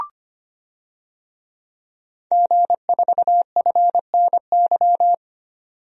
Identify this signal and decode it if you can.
Morse code, heard as a keyed tone, about 25 wpm, 700 Hz. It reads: G4FNY
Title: DXCC-100 Call Signs - 5 Characters - 25wpm